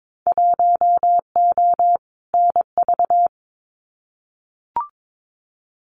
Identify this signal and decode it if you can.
Morse code, heard as a keyed tone, about 22 wpm, 700 Hz. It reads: 1O NV
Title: Winter Field Day Exchanges Encore 22wpm